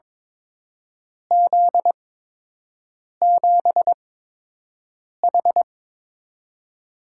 Morse code, heard as a keyed tone, about 22 wpm, spaced as for 6 wpm, 700 Hz. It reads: Z7H
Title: USA Call Signs - 4 Characters - 8x Character Spacing - 22wpm